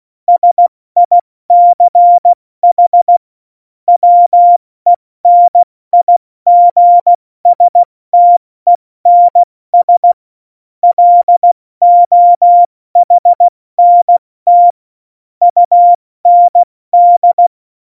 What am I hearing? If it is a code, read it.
Morse code, heard as a keyed tone, about 16 wpm, spaced as for 14 wpm, 700 Hz. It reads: SICH WENIGSTENS LOHNT UND
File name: Morsekurs / Witze / Witz 08. Dezember 2020, 70 F80